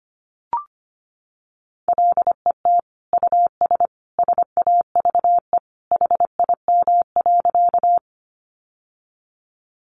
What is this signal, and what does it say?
Morse code, heard as a keyed tone, about 25 wpm, 700 Hz. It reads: LET US SAVE HIM.
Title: Sentences from Top 1100 Words 25wpm